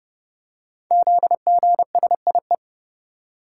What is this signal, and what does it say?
Morse code, heard as a keyed tone, about 30 wpm, 700 Hz. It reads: ZGSIE
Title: Code Groups - 5 Letters - 30wpm